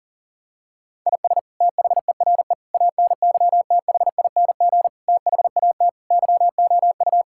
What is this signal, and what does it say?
Morse code, heard as a keyed tone, about 40 wpm, 700 Hz. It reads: IS THERE ANYTHING THAT YOU
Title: Sentences from Top 1000 Words - Review Part 6 - 40wpm